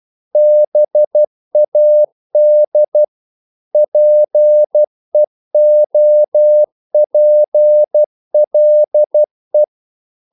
Morse code, heard as a keyed tone, about 12 wpm, 600 Hz. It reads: BAD PEOPLE
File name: CW / Practice / EnglishQuotations0112